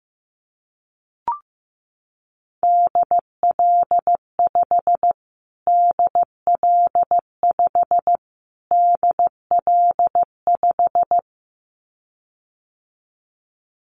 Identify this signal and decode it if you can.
Morse code, heard as a keyed tone, about 15 wpm, 700 Hz. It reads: DL5 DL5 DL5
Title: Callsign Prefixes for Top 25 DXCC - Repeated 3x - 15wpm